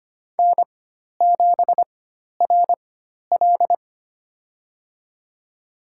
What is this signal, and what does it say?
Morse code, heard as a keyed tone, about 25 wpm, spaced as for 12 wpm, 700 Hz. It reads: N7RL
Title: USA Call Signs - 4 Characters - 4x Character Spacing - 25wpm